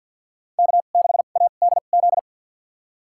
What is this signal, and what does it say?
Morse code, heard as a keyed tone, about 52 wpm, spaced as for 36 wpm, 700 Hz. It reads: K6ADZ
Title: USA Call Signs - 2x Character Spacing - 50wpm